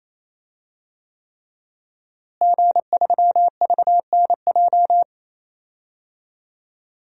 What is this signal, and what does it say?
Morse code, heard as a keyed tone, about 27 wpm, 700 Hz. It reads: G3VNJ
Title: DXCC-100 Call Signs - 5 Characters - Repeated 3x - 28wpm